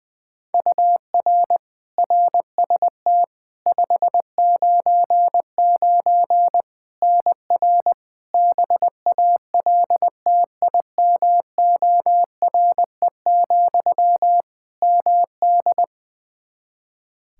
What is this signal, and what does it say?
Morse code, heard as a keyed tone, about 20 wpm, 700 Hz. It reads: UR RST 599 NR BALTIMORE, MD
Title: Practice QSO - Exchange 1 - Beginning - 20wpm